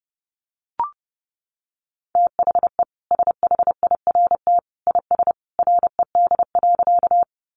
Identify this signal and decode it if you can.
Morse code, heard as a keyed tone, about 30 wpm, 700 Hz. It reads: THE SHIRT IS RED.
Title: Additional Easy Phrases 30wpm